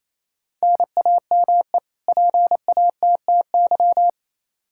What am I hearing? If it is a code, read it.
Morse code, heard as a keyed tone, about 28 wpm, 700 Hz. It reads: NAME PATTY
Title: K1USN SST QSO Practice 28wpm